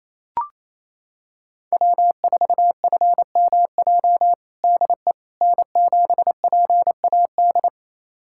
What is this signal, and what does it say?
Morse code, heard as a keyed tone, about 28 wpm, 700 Hz. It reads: W4FMJ DE N7PAD
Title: USA Callsign DE Callsign - 5 Characters - 28wpm